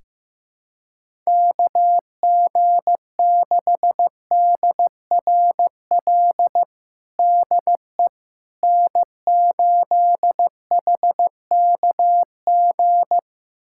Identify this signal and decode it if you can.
Morse code, heard as a keyed tone, about 15 wpm, 700 Hz. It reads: KG6DRL DE N8HKG
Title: USA Callsign DE Callsign 15wpm